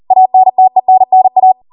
Morse code, heard as a keyed tone, about 40 wpm, 750 Hz. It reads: ANTENNA